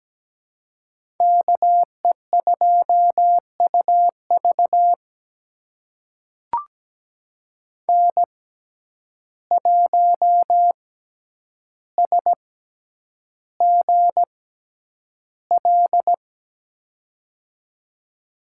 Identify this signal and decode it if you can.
Morse code, heard as a keyed tone, about 17 wpm, spaced as for 9 wpm, 700 Hz. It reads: KE2UV N1SGL
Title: USA Call Signs - 5 Characters - 6x Character Spacing - 17wpm